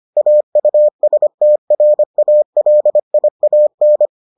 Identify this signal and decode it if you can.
Morse code, heard as a keyed tone, about 25 wpm, 600 Hz. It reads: AUSTRALIAN